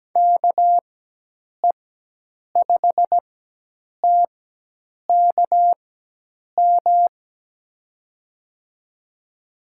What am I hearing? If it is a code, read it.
Morse code, heard as a keyed tone, about 17 wpm, spaced as for 8 wpm, 700 Hz. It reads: KE5TKM
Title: USA Call Signs - 6 Characters - 4x Character Spacing - 17wpm